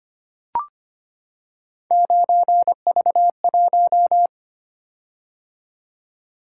Morse code, heard as a keyed tone, about 25 wpm, 700 Hz. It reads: 9V1
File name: Three Letter-Number Combinations - 25wpm